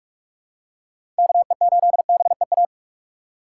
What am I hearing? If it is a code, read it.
Morse code, heard as a keyed tone, about 46 wpm, 700 Hz. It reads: KE8BEA
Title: USA Call Signs - 6 Characters - 6x Character Spacing - 45wpm